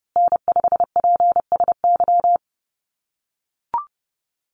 Morse code, heard as a keyed tone, about 30 wpm, 700 Hz. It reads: N5PSY
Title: USA Call Signs Repeated 2x 30wpm